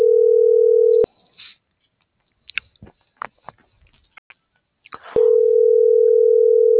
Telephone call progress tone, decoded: ringback tone

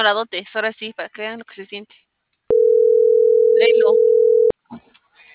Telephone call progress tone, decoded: ringback tone

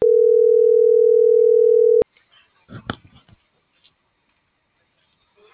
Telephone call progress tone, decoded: ringback tone

